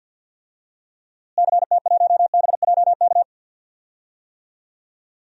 Morse code, heard as a keyed tone, about 47 wpm, 700 Hz. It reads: CT1BJK
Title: DXCC-100 Call Signs - 6 Characters - Repeated 2x - 50wpm